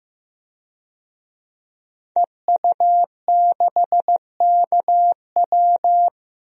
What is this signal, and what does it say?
Morse code, heard as a keyed tone, about 15 wpm, 700 Hz. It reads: EU6KW
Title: DXCC-100 Call Signs - 5 Characters - 15wpm